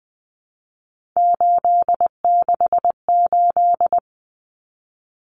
Morse code, heard as a keyed tone, about 20 wpm, 700 Hz. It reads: 868
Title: Numbers - 3-Digit 20wpm